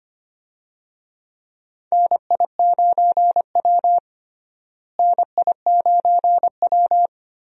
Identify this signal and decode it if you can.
Morse code, heard as a keyed tone, about 25 wpm, 700 Hz. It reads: NI9W NI9W
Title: USA Call Signs - 4 Characters - Repeated 3x - 25wpm